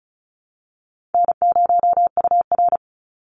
Morse code, heard as a keyed tone, about 34 wpm, 700 Hz. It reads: N0UR
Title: Call Signs Repeated 3x - 35wpm